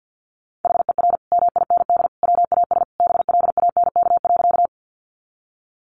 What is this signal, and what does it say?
Morse code, heard as a keyed tone, about 50 wpm, 700 Hz. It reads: HER MIND WAS BLANK.
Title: Sentences from Top 1500 Words 50wpm